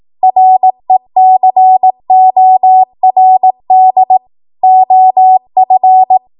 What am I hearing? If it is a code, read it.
Morse code, heard as a keyed tone, about 18 wpm, 750 Hz. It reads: RECORD OF
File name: Morse / 18WPM